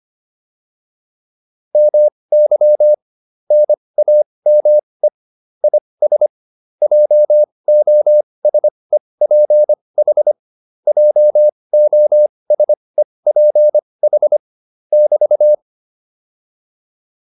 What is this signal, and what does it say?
Morse code, heard as a keyed tone, about 25 wpm, spaced as for 20 wpm, 600 Hz. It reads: MY NAME IS JOSEPH JOSEPH <BT>